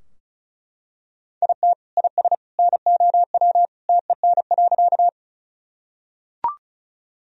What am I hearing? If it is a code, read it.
Morse code, heard as a keyed tone, about 36 wpm, 700 Hz. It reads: IT IS NOW TEN.
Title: Sentences from Top 1000 Words - Review Part 2 - 35wpm